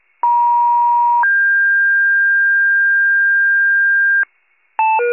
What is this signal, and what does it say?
Two-tone page, heard as a one-second tone, then a three-second tone, about 950 then 1600 Hz, with noise. It is heard from 0.2 to 4.2 s.